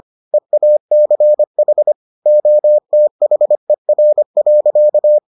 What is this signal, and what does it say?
Morse code, heard as a keyed tone, about 25 wpm, 600 Hz. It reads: EACH OTHER.